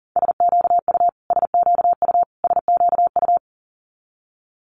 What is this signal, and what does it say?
Morse code, heard as a keyed tone, about 40 wpm, 700 Hz. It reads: SQU SQU SQU
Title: Multi-letter Phonemes Repeated 3x 40wpm